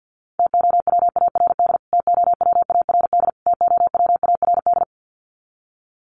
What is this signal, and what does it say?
Morse code, heard as a keyed tone, about 50 wpm, 700 Hz. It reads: TOWARD TOWARD TOWARD
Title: Top 500 Words Repeated 50wpm